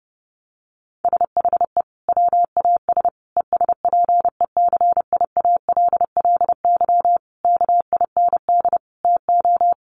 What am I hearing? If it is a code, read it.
Morse code, heard as a keyed tone, about 30 wpm, 700 Hz. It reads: SHE WAS ESPECIALLY KIND TO
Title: Sentences from Top 700 Words 30wpm